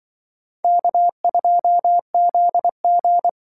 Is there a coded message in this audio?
Yes: Morse code from 0.6 to 3.3 s, heard as a keyed tone, about 25 wpm, 700 Hz. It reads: K2ZG